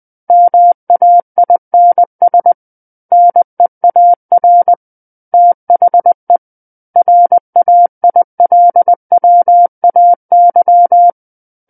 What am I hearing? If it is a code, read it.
Morse code, heard as a keyed tone, about 20 wpm, 700 Hz. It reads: MAINS NEAR THE RAILWAY
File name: 20 WPM